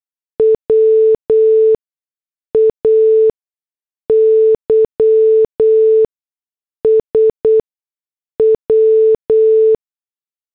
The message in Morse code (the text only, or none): WAYS W